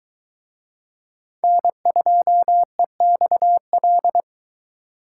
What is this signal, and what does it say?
Morse code, heard as a keyed tone, about 22 wpm, 700 Hz. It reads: N2EXL